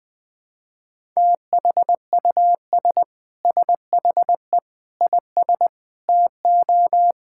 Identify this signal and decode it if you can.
Morse code, heard as a keyed tone, about 20 wpm, 700 Hz. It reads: THUS SHE IS TO
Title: Sentences from Top 1000 Words - Review Part 5 - 20wpm